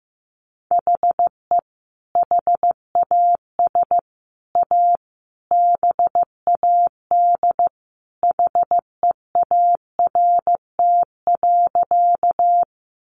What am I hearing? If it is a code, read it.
Morse code, heard as a keyed tone, about 15 wpm, 700 Hz. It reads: HE HAS A BAD HEART.
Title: Sentences from Top 1200 Words 15wpm